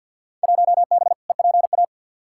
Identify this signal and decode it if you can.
Morse code, heard as a keyed tone, about 53 wpm, 700 Hz. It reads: 1B EPA
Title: Field Day Exchanges Encore - 50wpm